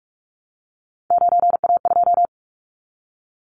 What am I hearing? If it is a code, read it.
Morse code, heard as a keyed tone, about 44 wpm, 700 Hz. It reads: 9A2
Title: Callsign Prefixes for Top 25 DXCC - Repeated 3x - 45wpm